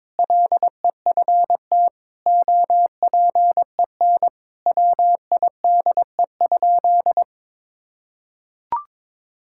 Morse code, heard as a keyed tone, about 22 wpm, 700 Hz. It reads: LEFT OPEN WIDE?
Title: Sentences from Top 1000 Words - Review Part 6 - 22wpm